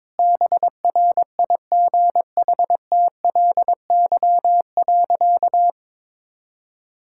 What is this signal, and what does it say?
Morse code, heard as a keyed tone, about 22 wpm, 700 Hz. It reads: BRIGHTLY.
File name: practice / 22wpm / Easy Sentences with Hints - 22wpm